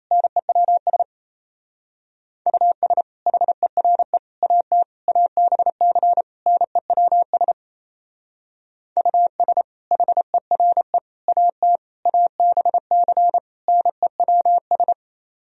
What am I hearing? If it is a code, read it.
Morse code, heard as a keyed tone, about 34 wpm, 700 Hz. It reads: NEWS US HERE AT ABC NEWS US HERE AT ABC NEWS
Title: Speed-Racing - Sets of 5 Words 28wpm